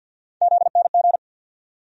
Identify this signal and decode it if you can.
Morse code, heard as a keyed tone, about 51 wpm, 700 Hz. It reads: ZNG
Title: USA Call Sign Postfixes - 3 Letters 50wpm